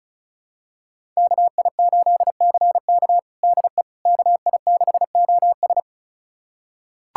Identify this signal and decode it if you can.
Morse code, heard as a keyed tone, about 34 wpm, 700 Hz. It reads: KI8CK DE KI6OS
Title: USA Callsign DE Callsign - 5 Characters - 35wpm